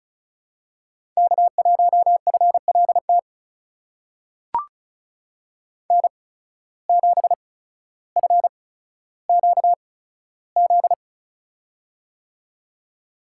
Morse code, heard as a keyed tone, about 36 wpm, spaced as for 15 wpm, 700 Hz. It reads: K1FLT N7FQZ
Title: USA Call Signs - 5 Characters - 8x Character Spacing - 35wpm